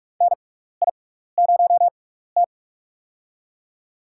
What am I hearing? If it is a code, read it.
Morse code, heard as a keyed tone, about 41 wpm, spaced as for 15 wpm, 700 Hz. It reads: NI0T
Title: USA Call Signs - 4 Characters - 6x Character Spacing - 45wpm